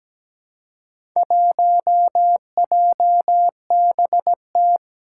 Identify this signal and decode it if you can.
Morse code, heard as a keyed tone, about 17 wpm, 700 Hz. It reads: 1JBT
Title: Four Challenging Letter-Number Combinations 17wpm